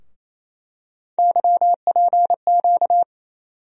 Morse code, heard as a keyed tone, about 28 wpm, 700 Hz. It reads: YPQ
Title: USA Call Sign Postfixes - 3 Letters 28wpm